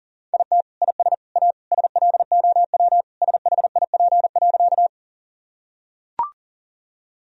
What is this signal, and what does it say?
Morse code, heard as a keyed tone, about 40 wpm, 700 Hz. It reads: IT IS A SLOW SHIP.
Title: Sentences from Top 1000 Words - Review Part 2 - 40wpm